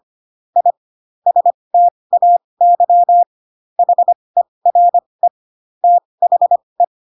Morse code, heard as a keyed tone, about 25 wpm, spaced as for 20 wpm, 700 Hz. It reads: I STAY HERE THE
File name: Pinocchio / Pinocchio ch4 15 25 ik1hge